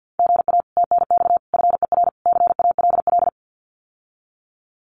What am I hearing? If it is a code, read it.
Morse code, heard as a keyed tone, about 50 wpm, 700 Hz. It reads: GA TNX FER CALL
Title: Common QSO Phrases Encore 50wpm